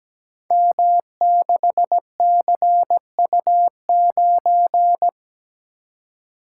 Morse code, heard as a keyed tone, about 17 wpm, 700 Hz. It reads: M6CU9
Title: Code Groups - 5 Letter:Numbers - 17wpm